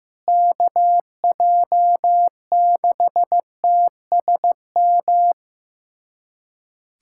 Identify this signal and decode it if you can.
Morse code, heard as a keyed tone, about 15 wpm, 700 Hz. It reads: KJ6TSM
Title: USA Callsign DE Callsign - 6 Characters - 15wpm